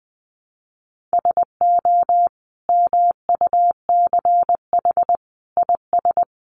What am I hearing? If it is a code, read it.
Morse code, heard as a keyed tone, about 20 wpm, 700 Hz. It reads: SO MUCH IS
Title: Sentences from Top 1200 Words 20wpm